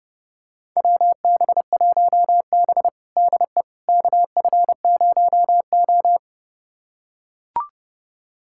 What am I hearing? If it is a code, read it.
Morse code, heard as a keyed tone, about 32 wpm, 700 Hz. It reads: WB1B DE KF0O
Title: USA Callsign DE Callsign - 4 Characters - 30wpm